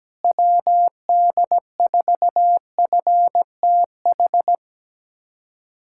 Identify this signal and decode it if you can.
Morse code, heard as a keyed tone, about 17 wpm, 700 Hz. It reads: WD4FTH
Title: USA Callsign DE Callsign - 6 Characters - 17wpm